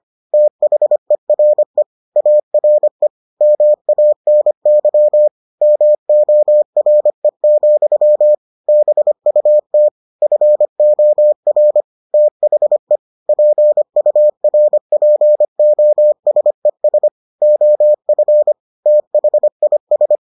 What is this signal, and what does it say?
Morse code, heard as a keyed tone, about 25 wpm, 600 Hz. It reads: THERE ARE MANY MORE, BUT FOR THE PURPOSES OF THIS